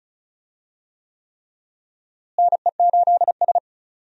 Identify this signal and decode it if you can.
Morse code, heard as a keyed tone, about 36 wpm, 700 Hz. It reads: NE8S
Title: USA Call Signs - 4 Characters - Repeated 3x - 35wpm